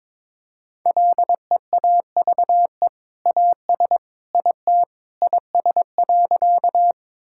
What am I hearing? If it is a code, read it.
Morse code, heard as a keyed tone, about 22 wpm, 700 Hz. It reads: LEAVE AS IT IS.
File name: practice / 22wpm / Sentences with 2-Word Phrases - 22wpm